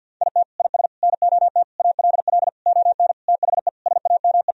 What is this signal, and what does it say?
Morse code, heard as a keyed tone, about 51 wpm, 700 Hz. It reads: IT IS NOT ALL ON THE SAME
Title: Sentences from Top 1000 Words - Review Part 3 - 50wpm